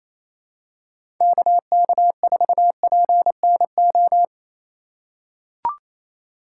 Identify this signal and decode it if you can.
Morse code, heard as a keyed tone, about 28 wpm, 700 Hz. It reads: KK4PNO
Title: USA Call Signs - 6 Characters - 4x Character Spacing - 28wpm